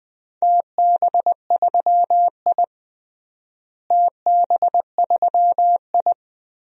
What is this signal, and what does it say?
Morse code, heard as a keyed tone, about 20 wpm, 700 Hz. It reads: TB3I TB3I
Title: DXCC-100 Call Signs - 4 Characters - Repeated 3x - 20wpm